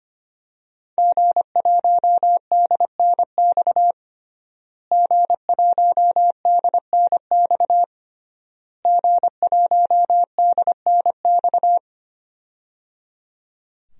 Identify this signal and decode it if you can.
Morse code, heard as a keyed tone, about 25 wpm, 700 Hz. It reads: G1DNX G1DNX G1DNX
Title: DXCC-100 Call Signs - 5 Characters - Repeated 3x - 25wpm